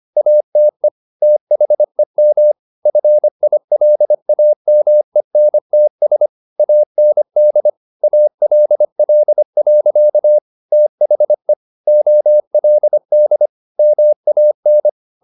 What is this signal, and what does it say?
Morse code, heard as a keyed tone, about 25 wpm, 600 Hz. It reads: ATE THEM FILAMENTS AND ALL. THE OLD MAN
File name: the old man and the sea part 1 25wpm